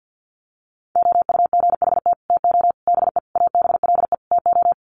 Morse code, heard as a keyed tone, about 50 wpm, 700 Hz. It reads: OUGHT TO BE ABLE TO
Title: Sets of 5 Words Encore2 50wpm